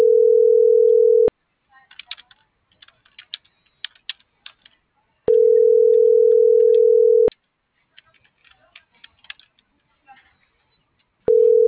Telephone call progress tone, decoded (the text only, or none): ringback tone